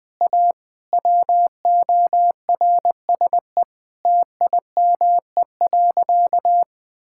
Morse code, heard as a keyed tone, about 20 wpm, 700 Hz. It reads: A WORSE TIME.
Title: Sentences with 8-Word Phrases - 20wpm